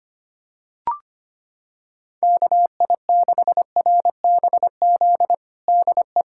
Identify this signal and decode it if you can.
Morse code, heard as a keyed tone, about 25 wpm, 700 Hz. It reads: KI6RBZ DE
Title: USA Callsign DE Callsign - 6 Characters - 25wpm